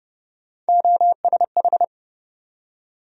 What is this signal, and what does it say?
Morse code, heard as a keyed tone, about 30 wpm, 700 Hz. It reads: OSH
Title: USA Call Sign Postfixes - 3 Letters 30wpm